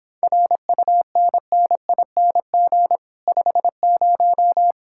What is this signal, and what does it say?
Morse code, heard as a keyed tone, about 31 wpm, 700 Hz. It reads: RUNNING 50